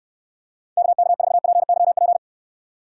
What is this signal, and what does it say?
Morse code, heard as a keyed tone, about 67 wpm, 700 Hz. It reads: KK4PYP